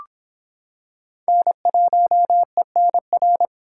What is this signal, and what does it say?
Morse code, heard as a keyed tone, about 25 wpm, 700 Hz. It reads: N1ENR